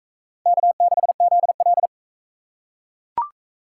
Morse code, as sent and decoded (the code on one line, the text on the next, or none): -.- -.... --.. .-..
K6ZL